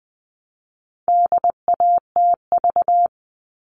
Morse code, heard as a keyed tone, about 20 wpm, 700 Hz. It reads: DATV
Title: All Cut Numbers - 4-Digit 20wpm